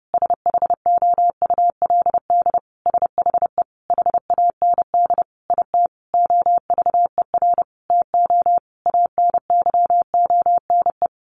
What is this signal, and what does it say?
Morse code, heard as a keyed tone, about 30 wpm, 700 Hz. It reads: SHOULD SHE HAND IT OVER TO ANYONE
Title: Sentences from Top 900 Words 30wpm